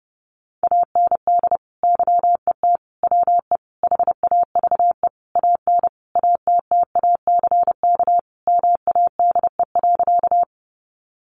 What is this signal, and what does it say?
Morse code, heard as a keyed tone, about 30 wpm, 700 Hz. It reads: AND YET WE HAVE AN ATTACK MADE.
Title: Sentences from Top 1500 Words 30wpm